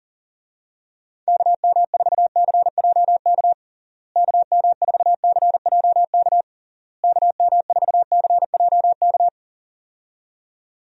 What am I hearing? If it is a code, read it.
Morse code, heard as a keyed tone, about 40 wpm, 700 Hz. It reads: KM4CJK KM4CJK KM4CJK
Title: USA Call Signs - 6 Characters - Repeated 3x - 40wpm